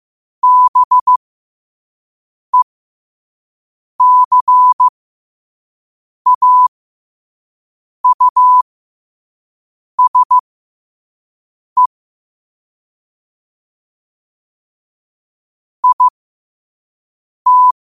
Morse code, heard as a keyed tone, about 15 wpm, spaced as for 5 wpm, 1000 Hz. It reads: BECAUSE IT
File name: qotdfiles / 5wpm